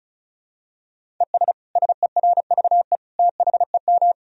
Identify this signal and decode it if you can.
Morse code, heard as a keyed tone, about 34 wpm, 700 Hz. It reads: ES SERVE THEM
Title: Sentences from Top 400 Words and Abbreviations 35wpm